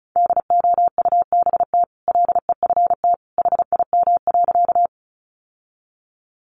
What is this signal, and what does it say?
Morse code, heard as a keyed tone, about 34 wpm, 700 Hz. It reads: DOUBT LEFT HIM.
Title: Sentences from Top 1500 Words 35wpm